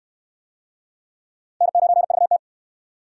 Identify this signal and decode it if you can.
Morse code, heard as a keyed tone, about 68 wpm, 700 Hz. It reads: N9VT